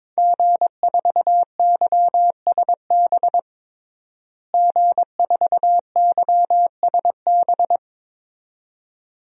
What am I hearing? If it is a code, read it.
Morse code, heard as a keyed tone, about 22 wpm, 700 Hz. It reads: G4YSB G4YSB
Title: DXCC-100 Call Signs - 5 Characters - Repeated 2x - 22wpm